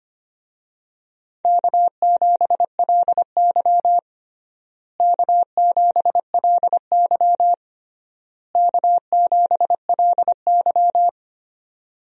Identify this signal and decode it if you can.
Morse code, heard as a keyed tone, about 25 wpm, 700 Hz. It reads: K7LY K7LY K7LY
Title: USA Call Signs - 4 Characters - Repeated 3x - 25wpm